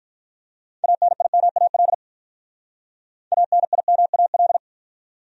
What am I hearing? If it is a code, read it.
Morse code, heard as a keyed tone, about 53 wpm, 700 Hz. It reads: ANIMAL ANIMAL